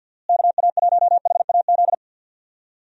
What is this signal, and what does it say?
Morse code, heard as a keyed tone, about 50 wpm, 700 Hz. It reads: KA1HAZ
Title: USA Call Signs - 6 Characters - Repeated 2x - 50wpm